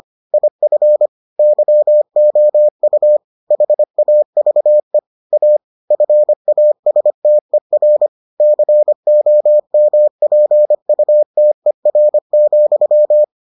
Morse code, heard as a keyed tone, about 25 wpm, 600 Hz. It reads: IF YOU HAVE A FASTER COMPUTER,